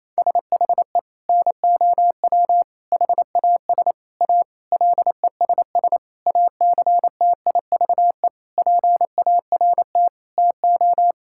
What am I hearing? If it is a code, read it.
Morse code, heard as a keyed tone, about 27 wpm, 700 Hz. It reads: SHE NOW HAS A LESS ACTIVE PART TO
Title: Sentences from Top 1000 Words - Review Part 6 - 28wpm